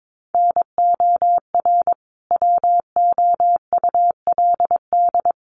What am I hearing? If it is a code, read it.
Morse code, heard as a keyed tone, about 22 wpm, 700 Hz. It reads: NOR WOULD
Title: Sentences from Top 1500 Words 22wpm